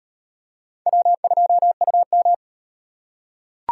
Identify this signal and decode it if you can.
Morse code, heard as a keyed tone, about 38 wpm, 700 Hz. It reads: W2UM